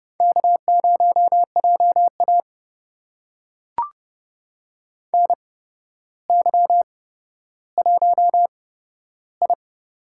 Morse code, heard as a keyed tone, about 31 wpm, spaced as for 13 wpm, 700 Hz. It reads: K0JA NY1I
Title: USA Call Signs - 4 Characters - 8x Character Spacing - 30wpm